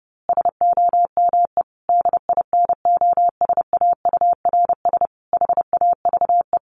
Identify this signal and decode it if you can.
Morse code, heard as a keyed tone, about 30 wpm, 700 Hz. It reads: SOME DINOSAURS HAVE
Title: A Sound Of Thunder 30wpm